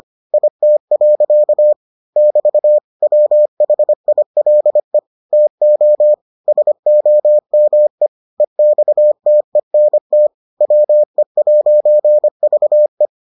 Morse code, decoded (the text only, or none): IT. <BT> WHILE TO SOME EXTENT WE'VE